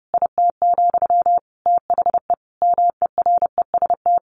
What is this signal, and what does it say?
Morse code, heard as a keyed tone, about 30 wpm, 700 Hz. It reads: IT, THE MEREST
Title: A Sound Of Thunder 30wpm